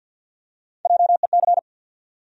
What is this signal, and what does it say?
Morse code, heard as a keyed tone, about 48 wpm, 700 Hz. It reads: JEC
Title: USA Call Sign Postfixes - 3 Letters 50wpm